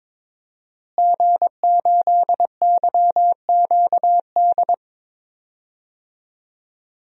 Morse code, heard as a keyed tone, about 22 wpm, 700 Hz. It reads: G8YQD
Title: DXCC-100 Call Signs - 5 Characters - Repeated 2x - 22wpm